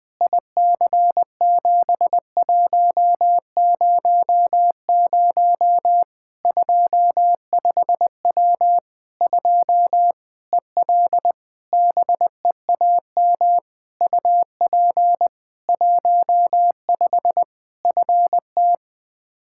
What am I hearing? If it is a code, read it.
Morse code, heard as a keyed tone, about 20 wpm, 700 Hz. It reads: IC7100 25W 2 EL BEAM UP 15 FT
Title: K1USN SST QSO Practice 20wpm